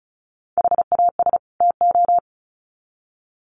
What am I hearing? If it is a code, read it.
Morse code, heard as a keyed tone, about 34 wpm, 700 Hz. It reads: HAS TO
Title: Sets of 2 Words 35wpm